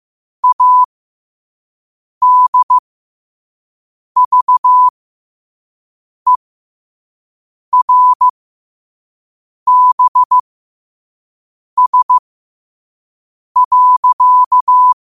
Morse code, heard as a keyed tone, about 15 wpm, spaced as for 5 wpm, 1000 Hz. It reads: ADVERBS.
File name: qotdfiles / 5wpm